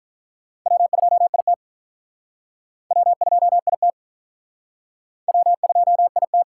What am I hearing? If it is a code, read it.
Morse code, heard as a keyed tone, about 45 wpm, 700 Hz. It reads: W2IT W2IT W2IT